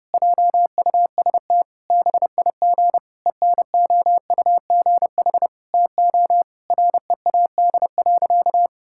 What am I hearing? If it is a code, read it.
Morse code, heard as a keyed tone, about 30 wpm, 700 Hz. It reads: JUST BIG ENOUGH TO READ.
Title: Sentences from Top 1000 Words - Review Part 5 - 30wpm